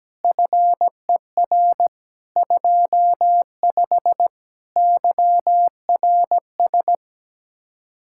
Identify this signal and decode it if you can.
Morse code, heard as a keyed tone, about 17 wpm, 700 Hz. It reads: FER 25 YRS
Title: K1USN SST QSO Practice 17wpm